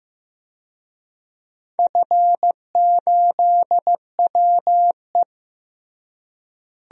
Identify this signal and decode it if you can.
Morse code, heard as a keyed tone, about 15 wpm, 700 Hz. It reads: F8WE
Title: DXCC-100 Call Signs - 4 Characters - Repeated 2x - 15wpm